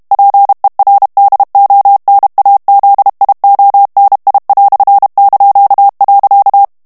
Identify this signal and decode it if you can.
Morse code, heard as a keyed tone, about 32 wpm, 800 Hz. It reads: PER DONAZIONI").
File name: morse mp3 / ITALIEN texte clair 1 vitesse 30 mots minute